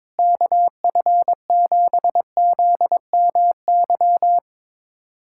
Morse code, heard as a keyed tone, about 21 wpm, 700 Hz. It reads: KF7ZMY
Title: USA Callsign DE Callsign 22wpm